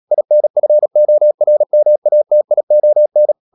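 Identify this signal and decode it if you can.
Morse code, heard as a keyed tone, about 37 wpm, 600 Hz. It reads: INFORMATION
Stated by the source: Morse code practice at 35–39 wpm sending English text of Ken Follett's novel Never, from CW med Gnister.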